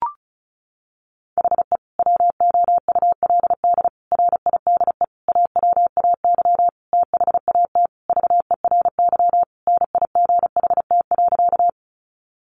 Morse code, heard as a keyed tone, about 34 wpm, 700 Hz. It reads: HE WOULD RIDE AWAY THAT VERY NIGHT.
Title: Sentences from Top 1200 Words 35wpm